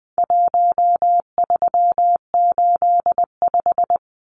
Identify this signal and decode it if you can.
Morse code, heard as a keyed tone, about 19 wpm, 700 Hz. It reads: 1385
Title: Numbers - 4-Digit 20wpm